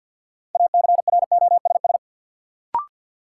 Morse code, heard as a keyed tone, about 49 wpm, 700 Hz. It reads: ACROSS
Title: Sets of 2 Words - New Word Drill - 50wpm